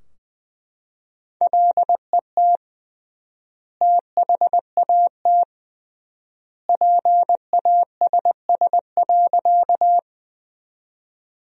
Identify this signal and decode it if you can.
Morse code, heard as a keyed tone, about 21 wpm, 700 Hz. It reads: LET THAT PASS.
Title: 3x Word-Spacing - Sentences from Top 400 Words - 20wpm